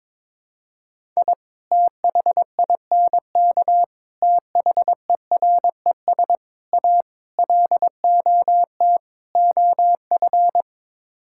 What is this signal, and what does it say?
Morse code, heard as a keyed tone, about 22 wpm, 700 Hz. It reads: I THINK THERES A LOT OF
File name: practice / 22wpm / Sets of 6 Words - 22wpm